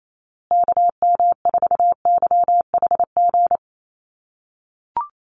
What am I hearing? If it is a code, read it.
Morse code, heard as a keyed tone, about 28 wpm, 700 Hz. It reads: KM4YHG
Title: USA Call Signs Repeated 2x 28wpm